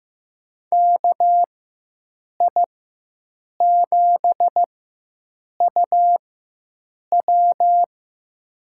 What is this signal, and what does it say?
Morse code, heard as a keyed tone, about 15 wpm, spaced as for 7 wpm, 700 Hz. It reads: KI7UW
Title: USA Call Signs - 5 Characters - 4x Character Spacing - 15wpm